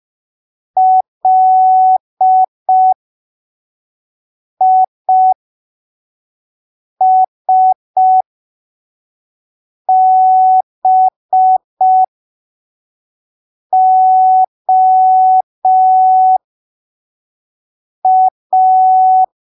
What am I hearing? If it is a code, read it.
Morse code, heard as a keyed tone, about 5 wpm, spaced as for 3 wpm, 750 Hz. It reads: LISBOA
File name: arquivos / texto / texto 16 05ppm